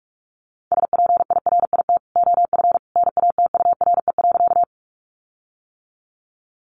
Morse code, heard as a keyed tone, about 44 wpm, 700 Hz. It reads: SPIRIT OF NATURE.
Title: Sentences from Top 1400 Words 45wpm